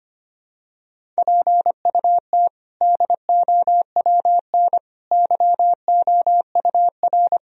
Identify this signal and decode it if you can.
Morse code, heard as a keyed tone, about 25 wpm, 700 Hz. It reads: PUT DOWN YOUR
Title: Sentences from Top 1000 Words - Review Part 1 - 25wpm